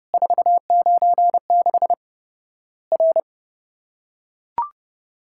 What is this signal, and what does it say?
Morse code, heard as a keyed tone, about 30 wpm, 700 Hz. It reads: 496 R
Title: Worked All Europe DX Contest - QTC Exchange - 30wpm